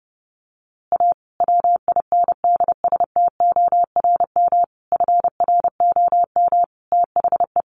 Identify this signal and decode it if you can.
Morse code, heard as a keyed tone, about 30 wpm, 700 Hz. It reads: A WINDSTORM FROM THE
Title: A Sound Of Thunder 30wpm